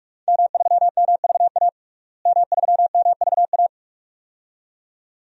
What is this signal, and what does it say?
Morse code, heard as a keyed tone, about 46 wpm, 700 Hz. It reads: M3MVA M3MVA
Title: DXCC-100 Call Signs - 5 Characters - Repeated 2x - 45wpm